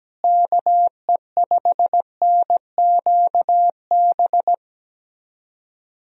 Morse code, heard as a keyed tone, about 17 wpm, 700 Hz. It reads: KE5NQB
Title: USA Call Signs - 6 Characters - Repeated 2x - 17wpm